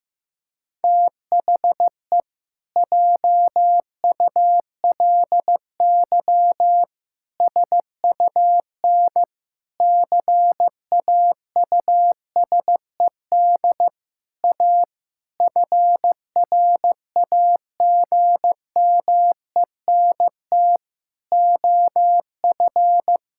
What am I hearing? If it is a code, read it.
Morse code, heard as a keyed tone, about 16 wpm, 700 Hz. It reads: THE JULY SUN CAUSED A FRAGMENT OF
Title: Pangram Challenge v2 - 15wpm